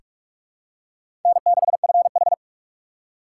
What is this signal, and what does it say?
Morse code, heard as a keyed tone, about 42 wpm, 700 Hz. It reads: N6FH
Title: USA Call Signs - 4 Characters - 4x Character Spacing - 45wpm